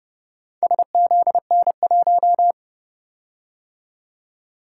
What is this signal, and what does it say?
Morse code, heard as a keyed tone, about 29 wpm, 700 Hz. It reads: SZN1